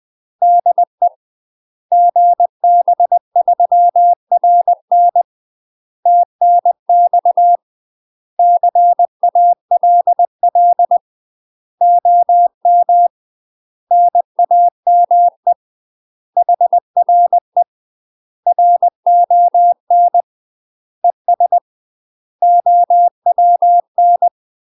Morse code, heard as a keyed tone, about 20 wpm, 700 Hz. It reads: DE GB3RN TNX CALL OM NAME HRE RON ES OWN
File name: UK MASTER QSOs TRAINING / 20WPM / UK MASTER QSOs 0028